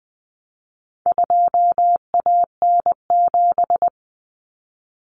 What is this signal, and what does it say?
Morse code, heard as a keyed tone, about 20 wpm, 700 Hz. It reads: 2AN7
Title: Common Cut Numbers - 4-Digit 20wpm